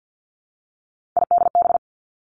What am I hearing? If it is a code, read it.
Morse code, heard as a keyed tone, about 50 wpm, 700 Hz. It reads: IDB
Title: Three Challenging Letter-Number Combinations 50wpm